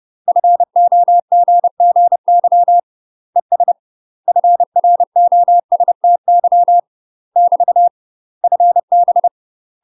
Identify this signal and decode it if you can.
Morse code, heard as a keyed tone, about 30 wpm, 700 Hz. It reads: FOGGY ES FROSTY <BT> FB
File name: UK MASTER QSOs TRAINING / 30WPM / UK MASTER QSOs 0011